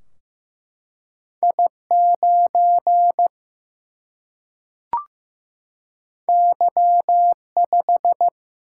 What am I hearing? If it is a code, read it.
Morse code, heard as a keyed tone, about 15 wpm, 700 Hz. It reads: I 9 Y 5
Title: Single-Letter Single-Number 15wpm